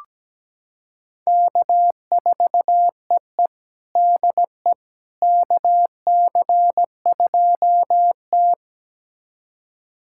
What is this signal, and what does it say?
Morse code, heard as a keyed tone, about 17 wpm, 700 Hz. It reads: K4EE DE KC2T
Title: USA Callsign DE Callsign - 4 Characters - 17wpm